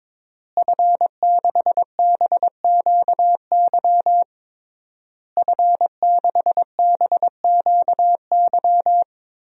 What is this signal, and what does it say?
Morse code, heard as a keyed tone, about 21 wpm, 700 Hz. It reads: F6BQY F6BQY
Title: DXCC-100 Call Signs - 5 Characters - Repeated 3x - 22wpm